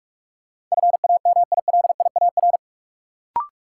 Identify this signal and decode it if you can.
Morse code, heard as a keyed tone, about 46 wpm, 700 Hz. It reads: FAMILIAR
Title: Sentences from Top 1000 Words - New Word Drill - 45wpm